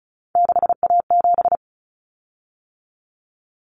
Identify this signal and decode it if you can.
Morse code, heard as a keyed tone, about 34 wpm, 700 Hz. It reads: 6A7
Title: Common Cut Numbers - 3-Digit 35wpm